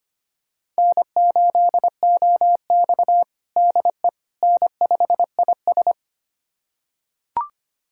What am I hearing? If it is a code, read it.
Morse code, heard as a keyed tone, about 25 wpm, 700 Hz. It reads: N8OX DE N5IS
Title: USA Callsign DE Callsign - 4 Characters - 25wpm